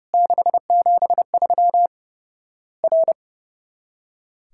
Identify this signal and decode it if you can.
Morse code, heard as a keyed tone, about 30 wpm, 700 Hz. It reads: 673 R
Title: Worked All Europe DX Contest - QTC Exchange - 30wpm